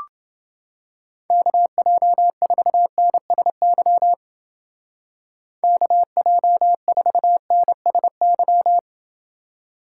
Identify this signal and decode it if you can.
Morse code, heard as a keyed tone, about 28 wpm, 700 Hz. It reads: KJ4NSY KJ4NSY